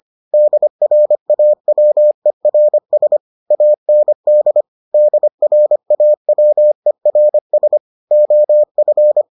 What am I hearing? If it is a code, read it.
Morse code, heard as a keyed tone, about 25 wpm, 600 Hz. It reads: DRAWERS AND DRAWERS OF